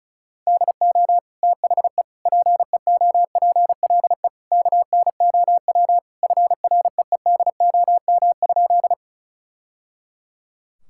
Morse code, heard as a keyed tone, about 36 wpm, 700 Hz. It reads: DO THE PEOPLE KNOW FREEDOM?
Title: Sentences from Top 1000 Words - Review Part 5 - 35wpm